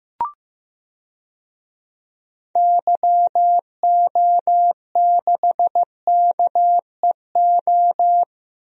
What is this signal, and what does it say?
Morse code, heard as a keyed tone, about 15 wpm, 700 Hz. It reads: YO6KEO
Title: DXCC-100 Call Signs - 6 Characters - 15wpm